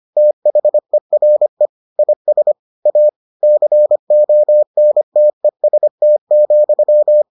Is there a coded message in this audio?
Yes: Morse code from 0.2 to 7.2 s, heard as a keyed tone, about 25 wpm, 600 Hz. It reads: THERE IS A CONTEST,